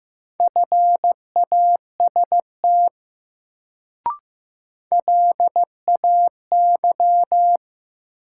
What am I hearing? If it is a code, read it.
Morse code, heard as a keyed tone, about 15 wpm, 700 Hz. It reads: FAST LAY
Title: Sentences from Top 400 Words - New Word Drill - 15wpm